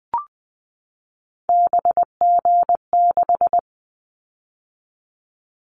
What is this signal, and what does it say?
Morse code, heard as a keyed tone, about 20 wpm, 700 Hz. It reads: BG6
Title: Callsign Prefixes for Top 100 DXCC - 20wpm